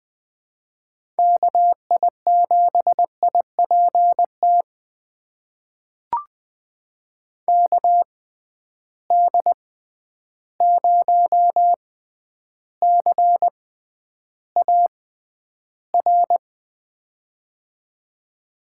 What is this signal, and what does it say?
Morse code, heard as a keyed tone, about 20 wpm, spaced as for 10 wpm, 700 Hz. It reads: KI7IPT KD0CAR
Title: USA Call Signs - 6 Characters - 6x Character Spacing - 20wpm